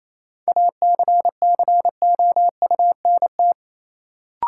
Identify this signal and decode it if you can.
Morse code, heard as a keyed tone, about 28 wpm, 700 Hz. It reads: ACCOUNT
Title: Sentences from Top 900 Words - New Word Drill - 28wpm